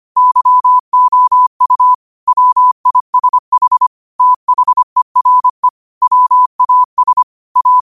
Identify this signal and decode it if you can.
Morse code, heard as a keyed tone, about 25 wpm, 1000 Hz. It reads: YOU WISH THERE WAS A
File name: qotdfiles / 25wpm